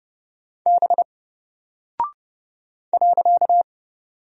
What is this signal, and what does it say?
Morse code, heard as a keyed tone, about 30 wpm, 700 Hz. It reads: B .
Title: Single Character (Letter:Number:Punctuation:Prosign) 30wpm